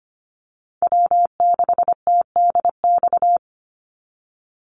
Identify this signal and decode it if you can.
Morse code, heard as a keyed tone, about 25 wpm, 700 Hz. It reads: W6TDX
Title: NA SOTA Chaser Callsigns 25wpm